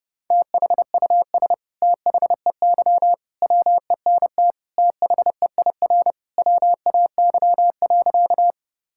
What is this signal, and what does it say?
Morse code, heard as a keyed tone, about 30 wpm, 700 Hz. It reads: THUS THEY WENT THEIR WAY.
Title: Sentences from Top 1000 Words - Review Part 5 - 30wpm